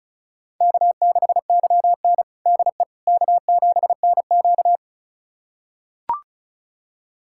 Morse code, heard as a keyed tone, about 36 wpm, 700 Hz. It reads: K6YN DE K7NQ
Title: USA Callsign DE Callsign - 4 Characters - 35wpm